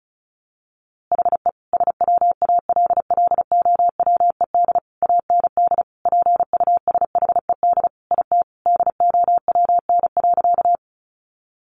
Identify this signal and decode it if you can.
Morse code, heard as a keyed tone, about 34 wpm, 700 Hz. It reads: HE SWALLOWED AND PUSHED IT DOWN.
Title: A Sound Of Thunder 35wpm